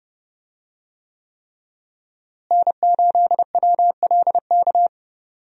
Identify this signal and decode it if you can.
Morse code, heard as a keyed tone, about 30 wpm, 700 Hz. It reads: N8WLK
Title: USA Call Signs - 5 Characters - Repeated 3x - 30wpm